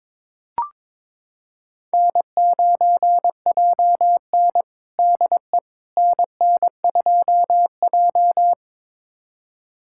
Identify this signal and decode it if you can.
Morse code, heard as a keyed tone, about 22 wpm, 700 Hz. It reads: N9JN DE NN2J
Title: USA Callsign DE Callsign - 4 Characters - 22wpm